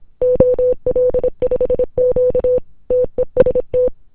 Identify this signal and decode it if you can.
Morse code, heard as a keyed tone, about 26 wpm, 510 Hz. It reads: OL5Q TEST
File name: ww01cw 160 ol5q